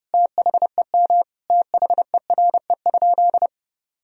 Morse code, heard as a keyed tone, about 30 wpm, 700 Hz. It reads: THEM THERE?
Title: Sentences from Top 1000 Words - Review Part 2 - 30wpm